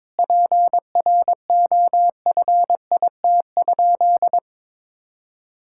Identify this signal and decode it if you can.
Morse code, heard as a keyed tone, about 22 wpm, 700 Hz. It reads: PROFIT?
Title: Sentences from Top 1000 Words - Review Part 5 - 22wpm